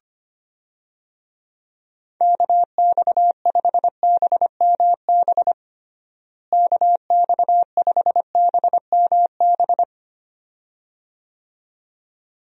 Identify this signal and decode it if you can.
Morse code, heard as a keyed tone, about 25 wpm, 700 Hz. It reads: KX5BMB KX5BMB
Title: USA Call Signs - 6 Characters - Repeated 2x - 25wpm